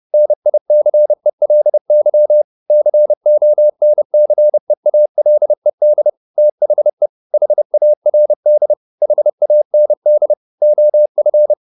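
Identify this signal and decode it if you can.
Morse code, heard as a keyed tone, about 30 wpm, 600 Hz. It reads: NICELY CONCEALED THE HARD HAND OF
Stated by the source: Morse code at 29 wpm, sending Ken Follett's novel Never in English, from CW med Gnister.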